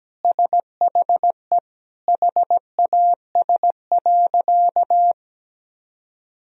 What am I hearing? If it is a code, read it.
Morse code, heard as a keyed tone, about 17 wpm, 700 Hz. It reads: SHE HAS.
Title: Sentences from Top 500 Words and Abbreviations 17wpm